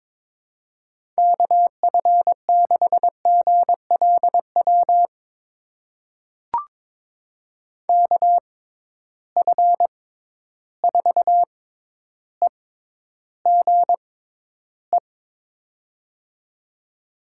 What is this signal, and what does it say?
Morse code, heard as a keyed tone, about 22 wpm, spaced as for 11 wpm, 700 Hz. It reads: KF6GLW KF4EGE
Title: USA Call Signs - 6 Characters - 6x Character Spacing - 22wpm